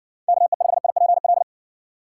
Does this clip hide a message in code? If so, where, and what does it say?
Yes: Morse code from 0.3 to 1.4 s, heard as a keyed tone, about 60 wpm, 700 Hz. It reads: KE5IPL